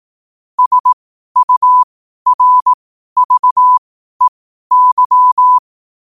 Morse code, heard as a keyed tone, about 18 wpm, spaced as for 12 wpm, 1000 Hz. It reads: SURVEY
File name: qotdfiles / 12wpm